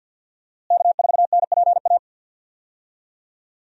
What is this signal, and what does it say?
Morse code, heard as a keyed tone, about 52 wpm, 700 Hz. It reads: K4NPA
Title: USA Call Signs - 5 Characters - Repeated 3x - 50wpm